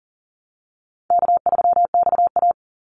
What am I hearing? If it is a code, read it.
Morse code, heard as a keyed tone, about 40 wpm, 700 Hz. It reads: K3XA
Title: Easy Call Signs Repeated 3x - 40wpm